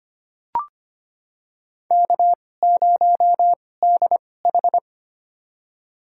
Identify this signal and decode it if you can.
Morse code, heard as a keyed tone, about 25 wpm, spaced as for 18 wpm, 700 Hz. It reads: K0DH
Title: USA Call Signs - 4 Characters - 2x Character Spacing - 25wpm